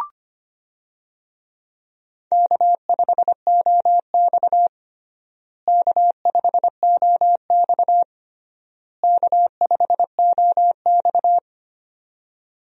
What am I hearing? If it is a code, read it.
Morse code, heard as a keyed tone, about 25 wpm, 700 Hz. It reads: K5OX K5OX K5OX
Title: USA Call Signs - 4 Characters - Repeated 3x - 25wpm